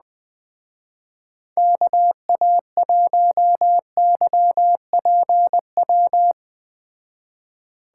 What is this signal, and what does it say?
Morse code, heard as a keyed tone, about 20 wpm, 700 Hz. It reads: KA1YPW
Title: Speed Racing - USA 2x3 Call Signs 17wpm